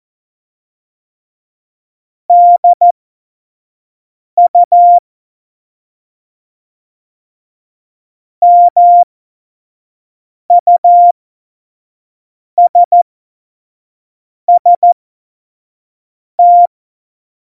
Morse code, heard as a keyed tone, about 14 wpm, spaced as for 5 wpm, 700 Hz. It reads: DU MUSST
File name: Morsekurs / Witze / Witz 26. Februar 2019, 25 F70